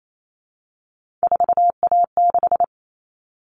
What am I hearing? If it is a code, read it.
Morse code, heard as a keyed tone, about 28 wpm, 700 Hz. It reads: 4A6
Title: All Cut Numbers - 3-Digit 28wpm